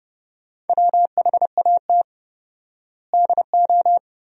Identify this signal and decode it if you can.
Morse code, heard as a keyed tone, about 32 wpm, 700 Hz. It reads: WHAT DO
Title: 4x Word-Spacing - Sentences from Top 200 Words - 30wpm